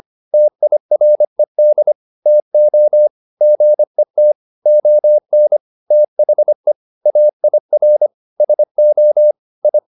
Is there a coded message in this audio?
Yes: Morse code from 0.3 to 9.8 s, heard as a keyed tone, about 25 wpm, 600 Hz. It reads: TIRED TO GET ON THE AIR SO I